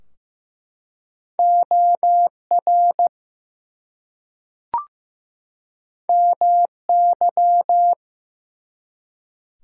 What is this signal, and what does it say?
Morse code, heard as a keyed tone, about 15 wpm, 700 Hz. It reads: OR MY
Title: Two Letter Words - 15wpm